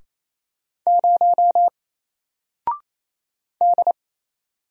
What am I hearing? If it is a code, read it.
Morse code, heard as a keyed tone, about 28 wpm, 700 Hz. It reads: 0 D
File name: Single-Letter-Number-28wpm